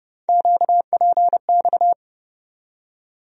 Morse code, heard as a keyed tone, about 30 wpm, 700 Hz. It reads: QPX
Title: USA Call Sign Postfixes - 3 Letters 30wpm